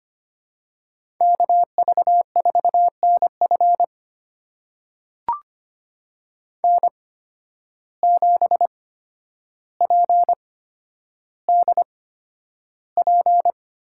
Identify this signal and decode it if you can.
Morse code, heard as a keyed tone, about 25 wpm, spaced as for 11 wpm, 700 Hz. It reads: KV4NF N7PDP
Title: USA Call Signs - 5 Characters - 8x Character Spacing - 25wpm